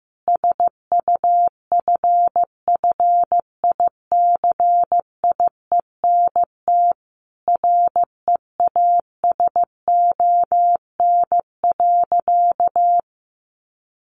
Morse code, heard as a keyed tone, about 15 wpm, 700 Hz. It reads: SUFFICIENT REASON.
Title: Sentences from Top 1400 Words 15wpm